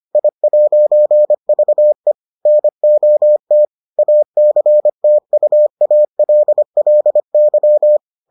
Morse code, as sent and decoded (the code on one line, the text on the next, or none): .. .----. ...- . / -. --- - / .- -.-. - ..- .- .-.. .-.. -.--
I'VE NOT ACTUALLY